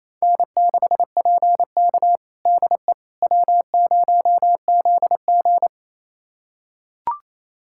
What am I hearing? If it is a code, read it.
Morse code, heard as a keyed tone, about 28 wpm, 700 Hz. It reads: N6PK DE W0ZG
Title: USA Callsign DE Callsign - 4 Characters - 28wpm